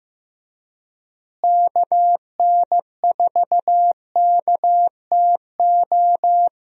Morse code, heard as a keyed tone, about 15 wpm, 700 Hz. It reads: KN4KTO
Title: USA Call Signs - 6 Characters - Repeated 3x - 15wpm